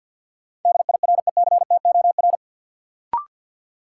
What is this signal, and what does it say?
Morse code, heard as a keyed tone, about 52 wpm, 700 Hz. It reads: DIRECTOR
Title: Sets of 3 Words - New Word Drill - 50wpm